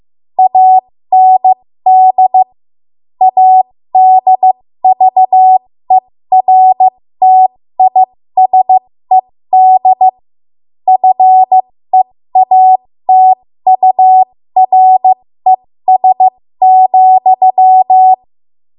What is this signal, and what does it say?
Morse code, heard as a keyed tone, about 15 wpm, spaced as for 13 wpm, 750 Hz. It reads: AND ADVERTISED FEATURES,
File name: Morse / 13WPM